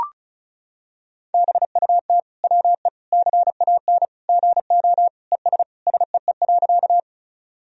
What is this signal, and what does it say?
Morse code, heard as a keyed tone, about 36 wpm, 700 Hz. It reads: BUT WE CAN GO ES SEE.
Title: Sentences from Top 100 Words and Abbreviations 35wpm